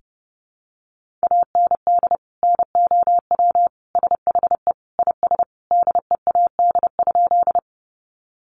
Morse code, heard as a keyed tone, about 30 wpm, 700 Hz. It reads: AND NOW SHE IS DEAD?
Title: Sentences from Top 1500 Words 30wpm